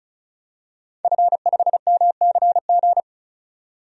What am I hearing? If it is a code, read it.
Morse code, heard as a keyed tone, about 34 wpm, 700 Hz. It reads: F5MCG
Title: DXCC-100 Call Signs - 5 Characters - 35wpm